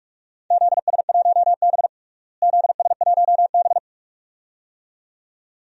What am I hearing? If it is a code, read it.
Morse code, heard as a keyed tone, about 46 wpm, 700 Hz. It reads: ZS1B ZS1B
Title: DXCC-100 Call Signs - 4 Characters - Repeated 2x - 45wpm